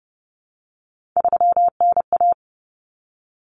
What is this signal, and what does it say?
Morse code, heard as a keyed tone, about 30 wpm, 700 Hz. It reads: 3NA
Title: Common Cut Numbers - 3-Digit 30wpm